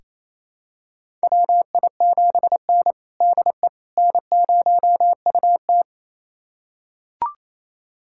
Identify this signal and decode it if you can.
Morse code, heard as a keyed tone, about 28 wpm, 690 Hz. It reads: WI7N DE N0UT
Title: USA Callsign DE Callsign - 4 Characters - 28wpm